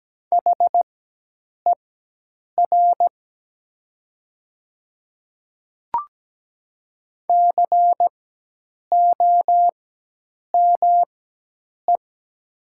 Morse code, heard as a keyed tone, about 17 wpm, spaced as for 8 wpm, 700 Hz. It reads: HER COME
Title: 4x Character Spacing - Top 100 Words - 17wpm